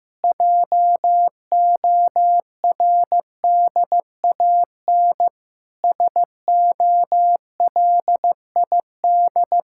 Morse code, heard as a keyed tone, about 15 wpm, 700 Hz. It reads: JORDAN SOLID
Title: Practice QSO - Exchange 3 - More Info - 15wpm